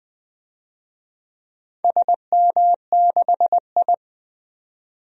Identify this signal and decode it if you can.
Morse code, heard as a keyed tone, about 20 wpm, 700 Hz. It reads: SM6I
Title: DXCC-100 Call Signs - 4 Characters - 20wpm